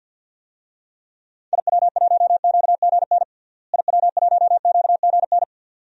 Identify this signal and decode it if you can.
Morse code, heard as a keyed tone, about 50 wpm, 700 Hz. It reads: IW1QGN IW1QGN
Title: DXCC-100 Call Signs - 6 Characters - Repeated 2x - 50wpm